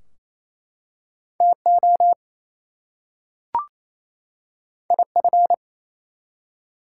Morse code, heard as a keyed tone, about 27 wpm, 700 Hz. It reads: TO IF
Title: Two Letter Words - 28wpm